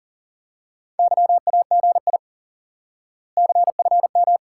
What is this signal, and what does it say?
Morse code, heard as a keyed tone, about 37 wpm, 700 Hz. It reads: YAGI CFM
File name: IQE Course 3 - Lesson 89 - YAGI - Two Elements - 40wpm